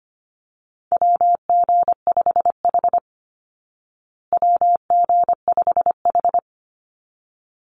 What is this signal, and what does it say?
Morse code, heard as a keyed tone, about 25 wpm, 700 Hz. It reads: WG5H WG5H
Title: Call Signs Repeated 3x - 25wpm